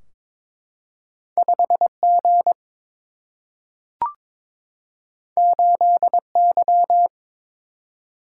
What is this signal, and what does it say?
Morse code, heard as a keyed tone, about 23 wpm, 700 Hz. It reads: 5 G 8 Y
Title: Single-Letter Single-Number 22wpm